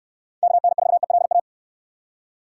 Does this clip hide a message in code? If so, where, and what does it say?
Yes: Morse code from 0.4 to 1.4 s, heard as a keyed tone, about 69 wpm, 700 Hz. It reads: KN4ELA